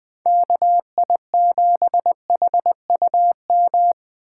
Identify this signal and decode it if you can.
Morse code, heard as a keyed tone, about 24 wpm, 700 Hz. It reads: KI7HUM
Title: Speed Racing - USA 2x3 Call Signs 20wpm